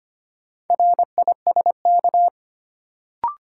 Morse code, heard as a keyed tone, about 25 wpm, 700 Hz. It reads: RISK
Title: Sentences from Top 600 Words - New Word Drill - 25wpm